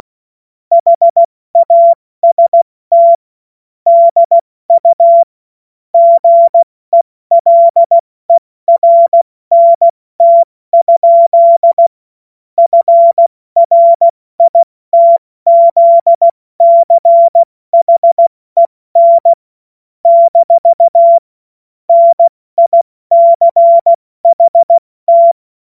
Morse code, heard as a keyed tone, about 16 wpm, 680 Hz. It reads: HAST DU GELERNT? FRITZCHEN - NICHT
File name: Morsekurs / Witze / Witz 10. August 2021, 70 F80